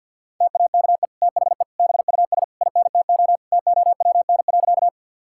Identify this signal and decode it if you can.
Morse code, heard as a keyed tone, about 50 wpm, 700 Hz. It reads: TAKE THE BUS INTO TOWN.
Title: Sentences from Top 1000 Words - Review Part 6 - 50wpm